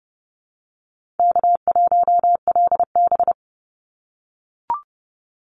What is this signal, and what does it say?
Morse code, heard as a keyed tone, about 30 wpm, 700 Hz. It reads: K1LB
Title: NA SOTA Chaser Callsigns 30wpm